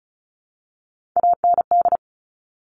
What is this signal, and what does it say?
Morse code, heard as a keyed tone, about 36 wpm, 700 Hz. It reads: AND
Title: Top Three Letter Combinations in Words 35wpm